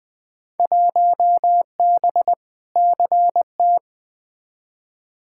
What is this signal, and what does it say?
Morse code, heard as a keyed tone, about 20 wpm, 700 Hz. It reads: 1B CT
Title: Field Day Exchanges Encore - 20wpm